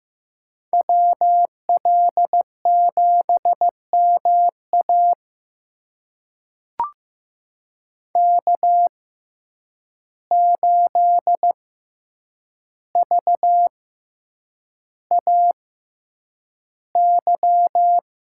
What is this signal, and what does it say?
Morse code, heard as a keyed tone, about 15 wpm, spaced as for 8 wpm, 700 Hz. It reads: WL7MA K8VAY
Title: USA Call Signs - 5 Characters - 6x Character Spacing - 15wpm